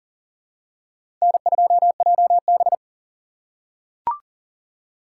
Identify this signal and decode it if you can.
Morse code, heard as a keyed tone, about 40 wpm, 700 Hz. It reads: N2JB
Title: USA Call Signs - 3x Character Spacing - 40wpm